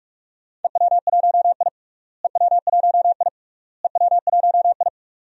Morse code, heard as a keyed tone, about 41 wpm, 700 Hz. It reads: EW1I EW1I EW1I
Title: DXCC-100 Call Signs - 4 Characters - Repeated 3x - 45wpm